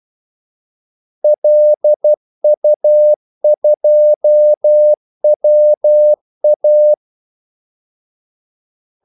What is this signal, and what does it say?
Morse code, heard as a keyed tone, about 12 wpm, 600 Hz. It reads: LU2WA
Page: 2 ekstra ordmellemrum mellem kaldesignalerne